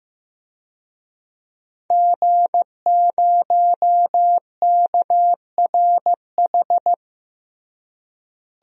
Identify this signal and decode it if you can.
Morse code, heard as a keyed tone, about 15 wpm, 700 Hz. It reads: G0KRH
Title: DXCC-100 Call Signs - 5 Characters - Repeated 2x - 15wpm